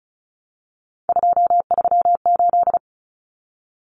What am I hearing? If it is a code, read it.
Morse code, heard as a keyed tone, about 34 wpm, 700 Hz. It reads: 238
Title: Common Cut Numbers - 3-Digit 35wpm